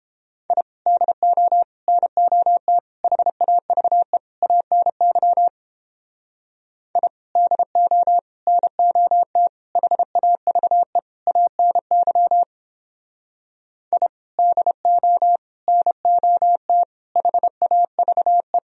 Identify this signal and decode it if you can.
Morse code, heard as a keyed tone, about 29 wpm, 700 Hz. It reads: I DO NOT HAVE ANY I DO NOT HAVE ANY I DO NOT HAVE